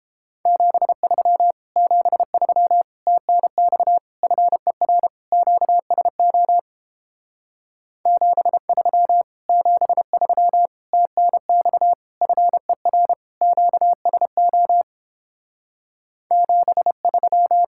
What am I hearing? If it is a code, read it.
Morse code, heard as a keyed tone, about 29 wpm, 700 Hz. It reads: 73 73 TNX FER QSO 73 73 TNX FER QSO 73